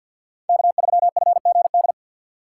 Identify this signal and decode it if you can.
Morse code, heard as a keyed tone, about 52 wpm, 700 Hz. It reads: K3FGD
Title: USA Call Signs - 6x Character Spacing - 50wpm